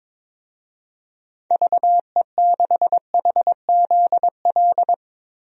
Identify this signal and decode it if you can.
Morse code, heard as a keyed tone, about 22 wpm, 700 Hz. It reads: VE6HZL
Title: DXCC-100 Call Signs - 6 Characters - 22wpm